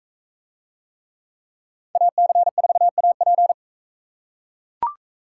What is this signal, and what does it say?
Morse code, heard as a keyed tone, about 42 wpm, 700 Hz. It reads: AC4AP